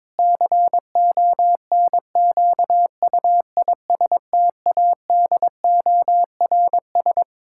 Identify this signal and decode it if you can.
Morse code, heard as a keyed tone, about 22 wpm, 700 Hz. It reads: CONQUISTADORS
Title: Pangram Challenge v2 - 22wpm